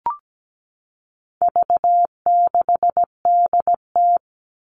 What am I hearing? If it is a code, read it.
Morse code, heard as a keyed tone, about 17 wpm, 700 Hz. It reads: V6DT
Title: All Cut Numbers - 4-Digit 17wpm